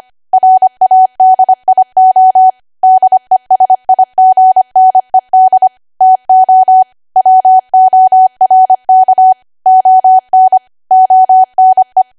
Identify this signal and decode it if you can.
Morse code, heard as a keyed tone, about 25 wpm, 750 Hz. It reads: RADIO DESIGNED TO WORK ON ONE